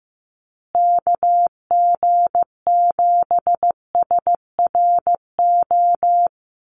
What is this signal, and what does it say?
Morse code, heard as a keyed tone, about 15 wpm, 700 Hz. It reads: KG7SRO
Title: USA Call Signs Repeated 3x 15wpm